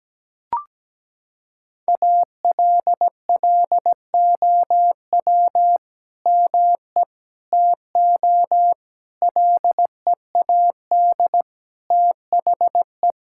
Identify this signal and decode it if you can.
Morse code, heard as a keyed tone, about 17 wpm, 700 Hz. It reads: ALLOW ME TO LEAD THE
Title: Sentences from Top 1000 Words - Review Part 5 - 17wpm